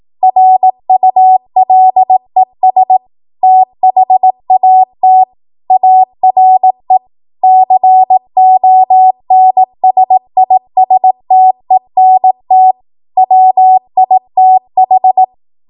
Morse code, heard as a keyed tone, about 18 wpm, 750 Hz. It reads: RULES THAT ARE CONSISTENT WITH
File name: Morse / 18WPM